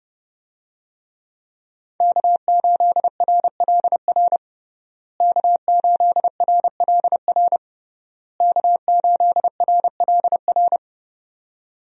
Morse code, heard as a keyed tone, about 32 wpm, 700 Hz. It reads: K8RLR K8RLR K8RLR
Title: USA Call Signs - 5 Characters - Repeated 3x - 30wpm